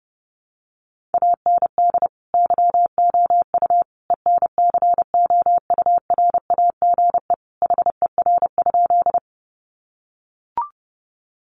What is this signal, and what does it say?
Morse code, heard as a keyed tone, about 30 wpm, 700 Hz. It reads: AND YOU ENCOURAGE HER?
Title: Sentences from Top 1400 Words 30wpm